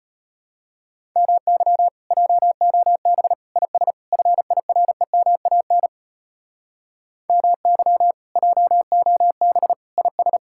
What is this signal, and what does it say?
Morse code, heard as a keyed tone, about 32 wpm, 700 Hz. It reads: MY JOB IS FIREMAN MY JOB IS